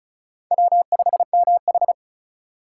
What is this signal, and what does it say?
Morse code, heard as a keyed tone, about 34 wpm, 700 Hz. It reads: W5MH
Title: USA Call Signs - 4 Characters - 8x Character Spacing - 35wpm